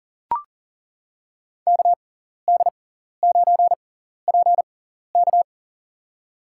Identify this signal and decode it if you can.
Morse code, heard as a keyed tone, about 40 wpm, spaced as for 14 wpm, 700 Hz. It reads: KD9PK
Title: USA Call Signs - 5 Characters - 6x Character Spacing - 40wpm